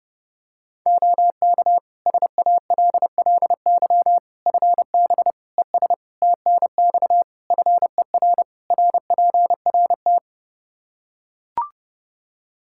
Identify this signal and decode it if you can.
Morse code, heard as a keyed tone, about 30 wpm, 700 Hz. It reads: OK SALLY FB ES TNX FER RPRT
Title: Practice QSO - Exchange 2 - Info - 30wpm